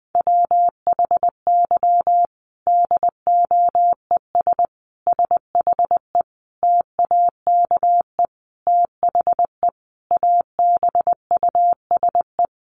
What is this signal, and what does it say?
Morse code, heard as a keyed tone, about 19 wpm, 700 Hz. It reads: WHY DOES SHE TAKE THE ABUSE
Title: Sentences from Top 1300 Words 20wpm